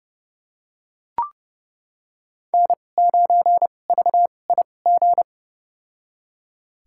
Morse code, heard as a keyed tone, about 30 wpm, spaced as for 22 wpm, 700 Hz. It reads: N9VIG
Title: USA Call Signs - 5 Characters - 2x Character Spacing - 30wpm